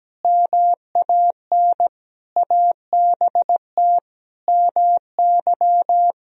Morse code, heard as a keyed tone, about 17 wpm, 700 Hz. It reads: MAN ABT MY
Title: Sentences from Top 400 Words and Abbreviations 17wpm